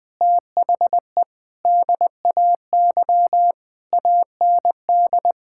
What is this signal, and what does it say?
Morse code, heard as a keyed tone, about 19 wpm, 700 Hz. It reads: THE DAY AND
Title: Sentences with 8-Word Phrases - 20wpm